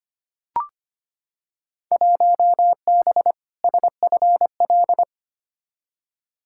Morse code, heard as a keyed tone, about 25 wpm, 700 Hz. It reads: 1B SFL
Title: Field Day Exchanges Encore - 25wpm